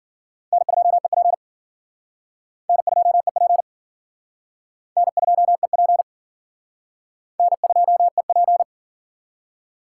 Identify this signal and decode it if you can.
Morse code, heard as a keyed tone, about 53 wpm, 700 Hz. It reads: N2EP N2EP N2EP N2EP